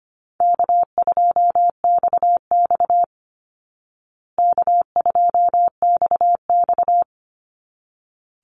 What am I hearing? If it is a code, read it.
Morse code, heard as a keyed tone, about 25 wpm, 700 Hz. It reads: K2XX K2XX
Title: Call Signs Repeated 3x - 25wpm